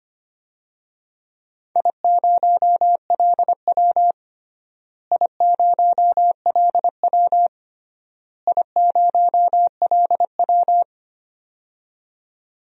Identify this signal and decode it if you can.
Morse code, heard as a keyed tone, about 25 wpm, 700 Hz. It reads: I0LW I0LW I0LW
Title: DXCC-100 Call Signs - 4 Characters - Repeated 3x - 25wpm